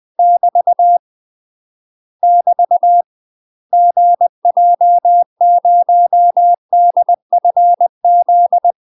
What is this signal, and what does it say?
Morse code, heard as a keyed tone, about 20 wpm, 700 Hz. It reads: <BT> <BT> GJ0DFZ
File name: UK MASTER QSOs TRAINING / 20WPM / UK MASTER QSOs 0042